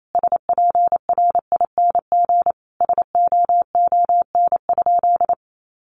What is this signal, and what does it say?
Morse code, heard as a keyed tone, about 28 wpm, 700 Hz. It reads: SPRING SOON?
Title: Sentences from Top 1200 Words 28wpm